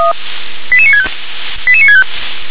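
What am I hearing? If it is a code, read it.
Five-tone selective call (ZVEI), heard as quick runs of five tones, tones of about 70 ms. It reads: 88965 88965
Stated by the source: ZVEI five-tone selective call for address 88960, received by SDR.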